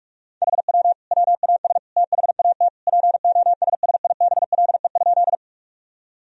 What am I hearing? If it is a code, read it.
Morse code, heard as a keyed tone, about 46 wpm, 700 Hz. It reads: HW WAS THAT POSSIBLE?
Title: Sentences from Top 500 Words and Abbreviations 45wpm